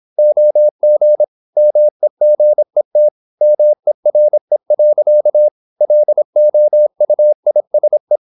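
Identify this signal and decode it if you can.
Morse code, heard as a keyed tone, about 26 wpm, 600 Hz. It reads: OG MEGET MERE. LOUISE
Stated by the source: High-speed (QRQ) Morse at 26 wpm.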